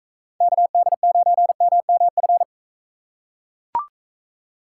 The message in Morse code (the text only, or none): KD9MMF